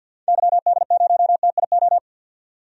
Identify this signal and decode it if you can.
Morse code, heard as a keyed tone, about 49 wpm, 700 Hz. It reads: YD0TIO
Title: DXCC-100 Call Signs - 6 Characters - 50wpm